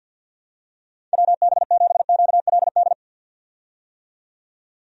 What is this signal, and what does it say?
Morse code, heard as a keyed tone, about 47 wpm, 700 Hz. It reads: WB7QLD
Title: USA Call Signs - 6 Characters - 50wpm